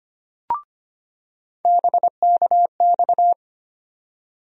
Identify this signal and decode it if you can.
Morse code, heard as a keyed tone, about 25 wpm, 700 Hz. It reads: BKX
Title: USA Call Sign Postfixes - 3 Letters 25wpm